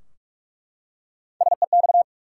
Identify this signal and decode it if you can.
Morse code, heard as a keyed tone, about 42 wpm, 700 Hz. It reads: SEX
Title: Three Letter Words - 45wpm